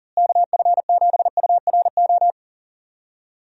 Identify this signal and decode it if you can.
Morse code, heard as a keyed tone, about 38 wpm, 700 Hz. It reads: KF7URO
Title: USA Call Signs - 6 Characters - Repeated 2x - 40wpm